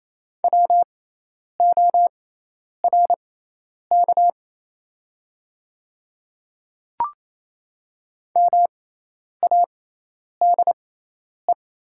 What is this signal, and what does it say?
Morse code, heard as a keyed tone, about 28 wpm, spaced as for 10 wpm, 700 Hz. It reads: WORK MADE
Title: 6x Character Spacing - Top 300 Words - 28wpm